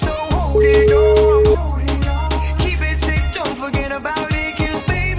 Ringback tone (Europe, 425 Hz), with noise.